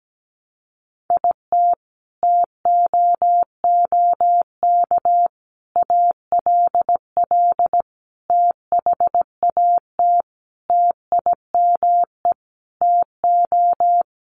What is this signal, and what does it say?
Morse code, heard as a keyed tone, about 17 wpm, 700 Hz. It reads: IT TOOK ALL THAT TIME TO
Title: Sentences from Top 300 Words 17wpm